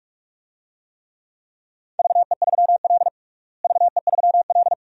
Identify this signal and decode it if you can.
Morse code, heard as a keyed tone, about 41 wpm, 700 Hz. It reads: VE3L VE3L
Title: DXCC-100 Call Signs - 4 Characters - Repeated 2x - 45wpm